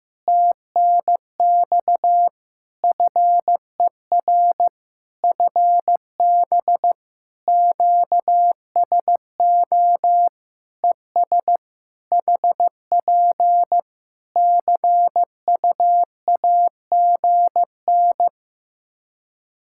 Morse code, heard as a keyed tone, about 15 wpm, 700 Hz. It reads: TNX FER FB QSO ES HP CUAGN
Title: Practice QSO - Exchange 4 - Ending - 15wpm